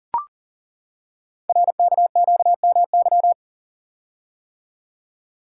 Morse code, heard as a keyed tone, about 39 wpm, 690 Hz. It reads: RKQMY
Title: Code Groups - 5 Letter:Numbers - 40wpm